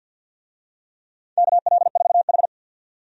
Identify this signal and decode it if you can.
Morse code, heard as a keyed tone, about 51 wpm, 700 Hz. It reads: KL4H
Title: USA Call Signs - 4 Characters - 50wpm